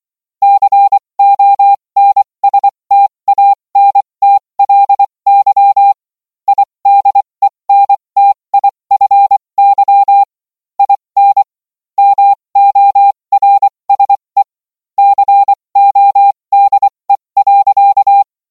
Morse code, heard as a keyed tone, about 24 wpm, spaced as for 20 wpm, 800 Hz. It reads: CONSTANTLY IDENTIFY IN MORSE CODE.